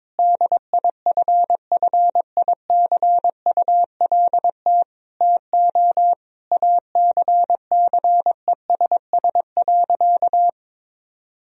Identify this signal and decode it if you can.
Morse code, heard as a keyed tone, about 21 wpm, 700 Hz. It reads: DIFFICULT TO ACCESS.
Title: Sentences from Top 1000 Words - Review Part 4 - 22wpm